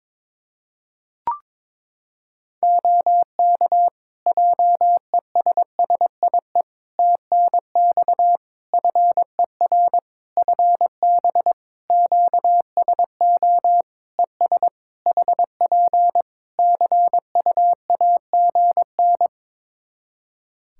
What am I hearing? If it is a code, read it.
Morse code, heard as a keyed tone, about 22 wpm, 700 Hz. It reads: OK JESSIE TNX FER FB QSO ES HP CUAGN
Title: Practice QSO - Exchange 4 - Ending - 22wpm